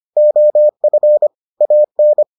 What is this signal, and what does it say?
Morse code, heard as a keyed tone, about 25 wpm, 600 Hz. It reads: OF AN